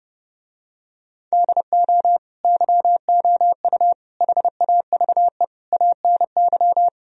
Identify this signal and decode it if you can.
Morse code, heard as a keyed tone, about 30 wpm, 700 Hz. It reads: DO YOU HAVE ANY